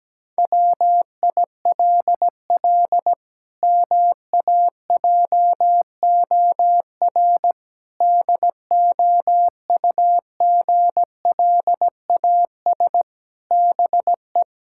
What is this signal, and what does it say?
Morse code, heard as a keyed tone, about 17 wpm, 700 Hz. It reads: WILL MAJOR DOUGLAS BE
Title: Pangram Challenge v2 - 17wpm